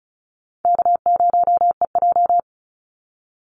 Morse code, heard as a keyed tone, about 34 wpm, 700 Hz. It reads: K0EJ
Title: Call Signs - 35wpm